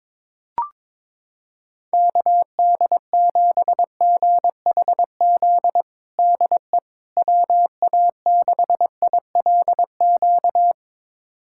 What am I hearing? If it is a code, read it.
Morse code, heard as a keyed tone, about 22 wpm, 700 Hz. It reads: KD7GHZ DE WA6ILQ
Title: USA Callsign DE Callsign - 6 Characters - 22wpm